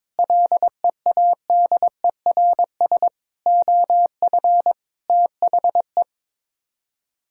Morse code, heard as a keyed tone, about 22 wpm, 700 Hz. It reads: LEADERS OF THE
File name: practice / 22wpm / Sets of 6 Words - 22wpm